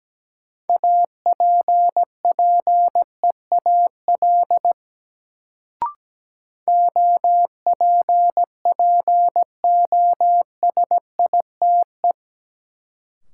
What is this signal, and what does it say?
Morse code, heard as a keyed tone, about 17 wpm, 700 Hz. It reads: APPEAL OPPOSITE
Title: Sentences from Top 1300 Words - New Word Drill - 17wpm